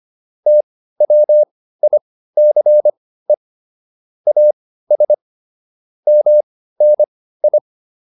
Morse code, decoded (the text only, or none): TWICE AS MNI